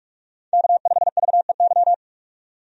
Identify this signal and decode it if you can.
Morse code, heard as a keyed tone, about 46 wpm, 700 Hz. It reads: K5VEY
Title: USA Call Signs - 5 Characters - 45wpm